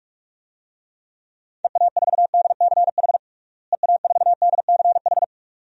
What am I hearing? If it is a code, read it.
Morse code, heard as a keyed tone, about 46 wpm, 700 Hz. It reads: EA4DCH EA4DCH
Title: DXCC-100 Call Signs - 6 Characters - Repeated 2x - 45wpm